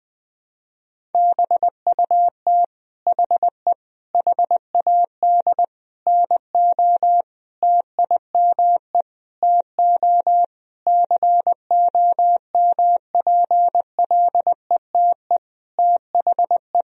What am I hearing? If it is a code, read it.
Morse code, heard as a keyed tone, about 20 wpm, 700 Hz. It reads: BUT HE HAD NO TIME TO COMPLETE THE
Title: Sentences from Top 1000 Words - Review Part 1 - 20wpm